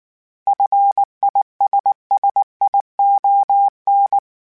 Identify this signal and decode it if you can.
Morse code, heard as a keyed tone, about 19 wpm, 800 Hz. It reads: FISSION